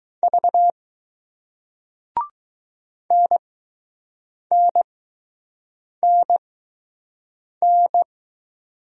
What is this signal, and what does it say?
Morse code, heard as a keyed tone, about 20 wpm, spaced as for 6 wpm, 700 Hz. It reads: V NNNN